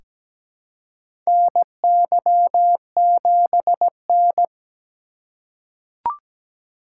Morse code, heard as a keyed tone, about 17 wpm, 700 Hz. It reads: NY7N
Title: K1USN SST QSO Practice - 17wpm